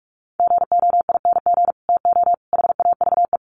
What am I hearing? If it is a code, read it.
Morse code, heard as a keyed tone, about 44 wpm, 700 Hz. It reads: GOING TO HAVE
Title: Sets of 5 Words Encore 45wpm